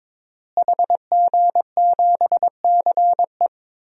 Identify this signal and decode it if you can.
Morse code, heard as a keyed tone, about 22 wpm, 700 Hz. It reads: HG7CE
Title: DXCC-100 Call Signs - 5 Characters - Repeated 3x - 22wpm